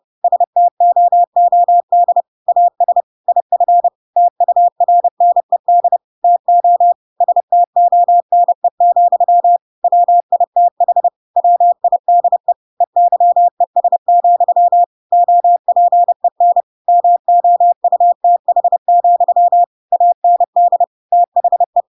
Morse code, heard as a keyed tone, about 30 wpm, 700 Hz. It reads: STOOD AS IF TURNED TO STONE, WITH WIDE EYES, OPEN MOUTH, AND THE
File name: Pinocchio / Pinocchio ch5 30 30 ik1hge